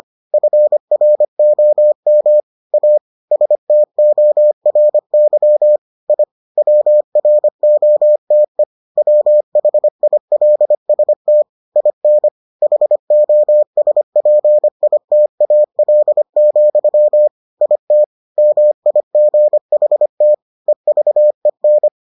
Morse code, heard as a keyed tone, about 25 wpm, 600 Hz. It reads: FROM A STORY I WROTE WHILST IN HOSPITAL, IT MIGHT EVEN